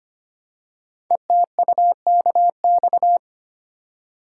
Morse code, heard as a keyed tone, about 25 wpm, 700 Hz. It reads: ETUKX
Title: Code Groups - 5 Letters - 25wpm